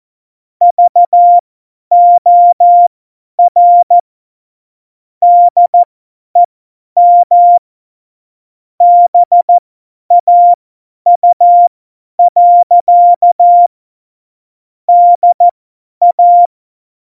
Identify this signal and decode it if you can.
Morse code, heard as a keyed tone, about 14 wpm, spaced as for 10 wpm, 700 Hz. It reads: VOR DEM BAU. DA
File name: Morsekurs / Witze / Witz 03. November 2020, 50 F70